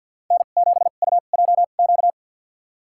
Morse code, heard as a keyed tone, about 53 wpm, spaced as for 37 wpm, 700 Hz. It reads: N7UJQ
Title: USA Call Signs - 5 Characters - 2x Character Spacing - 50wpm